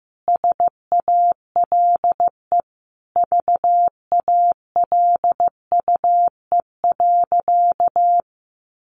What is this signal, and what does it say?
Morse code, heard as a keyed tone, about 15 wpm, 700 Hz. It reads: SALE VALUE.
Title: Sentences from Top 1400 Words 15wpm